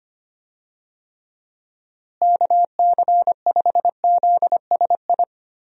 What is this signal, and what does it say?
Morse code, heard as a keyed tone, about 25 wpm, 700 Hz. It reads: KC5ZSI
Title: USA Call Signs - 6 Characters - Repeated 2x - 25wpm